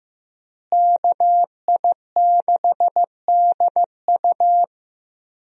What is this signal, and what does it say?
Morse code, heard as a keyed tone, about 15 wpm, 700 Hz. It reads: KI6DU
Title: USA Call Signs - 5 Characters - Repeated 2x - 15wpm